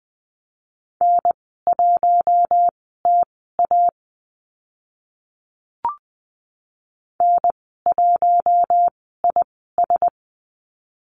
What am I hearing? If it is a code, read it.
Morse code, heard as a keyed tone, about 20 wpm, spaced as for 14 wpm, 700 Hz. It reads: N1TA N1IS
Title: 042 Lesson 8g - Number 1 - Call Signs 20wpm